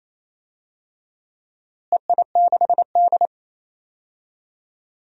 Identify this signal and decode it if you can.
Morse code, heard as a keyed tone, about 28 wpm, 700 Hz. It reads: EI6D
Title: DXCC-100 Call Signs - 4 Characters - Repeated 2x - 28wpm